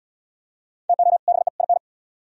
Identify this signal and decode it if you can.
Morse code, heard as a keyed tone, about 67 wpm, 700 Hz. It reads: TO BE IN